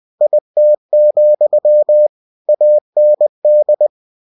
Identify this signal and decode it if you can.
Morse code, heard as a keyed tone, about 20 wpm, 600 Hz. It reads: IT, AND
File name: cw-books / winnie-the-pooh / winnie-20wpm-page-0022